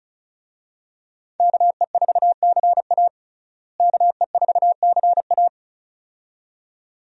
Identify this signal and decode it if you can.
Morse code, heard as a keyed tone, about 36 wpm, 700 Hz. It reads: KE4CA KE4CA
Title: USA Call Signs - 5 Characters - Repeated 2x - 35wpm